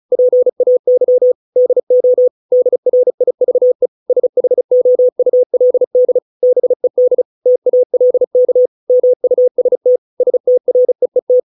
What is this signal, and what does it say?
Morse code, heard as a keyed tone, about 36 wpm, 500 Hz. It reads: PAY DO DRIVE SHOULD BED TALK MUST STREET